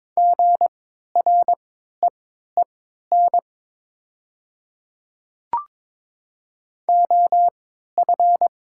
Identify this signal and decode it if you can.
Morse code, heard as a keyed tone, about 22 wpm, spaced as for 12 wpm, 690 Hz. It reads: GREEN OF
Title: 3x Character Spacing - Top 500 Words - 22wpm